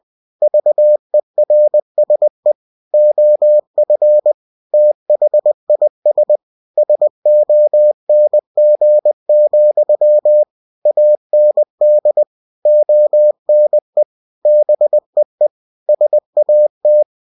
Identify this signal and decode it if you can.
Morse code, heard as a keyed tone, about 20 wpm, 600 Hz. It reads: VERSE OF THIS SONG, AND ONE BEE SAT
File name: cw-books / winnie-the-pooh / winnie-20wpm-page-0009